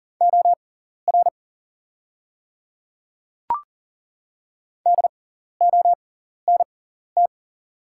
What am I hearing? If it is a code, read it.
Morse code, heard as a keyed tone, about 40 wpm, spaced as for 14 wpm, 700 Hz. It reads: OR DONT
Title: 6x Character Spacing - Top 300 Words - 40wpm